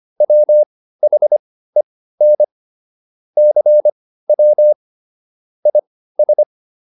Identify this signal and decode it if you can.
Morse code, heard as a keyed tone, about 25 wpm, spaced as for 15 wpm, 600 Hz. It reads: WHEN CW IS